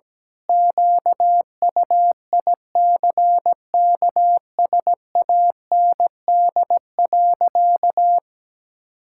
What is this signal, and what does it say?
Morse code, heard as a keyed tone, about 17 wpm, 700 Hz. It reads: QUICKSAND.
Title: Pangram Challenge v2 - 17wpm